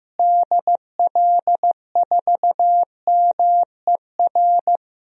Morse code, heard as a keyed tone, about 15 wpm, 700 Hz. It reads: DL4MER
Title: DXCC-100 Call Signs - 6 Characters - Repeated 2x - 15wpm